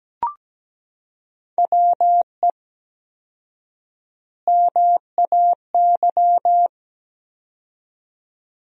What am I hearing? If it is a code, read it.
Morse code, heard as a keyed tone, about 17 wpm, 700 Hz. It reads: WE MAY
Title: 4x Word-Spacing - Sentences from Top 300 Words - 17wpm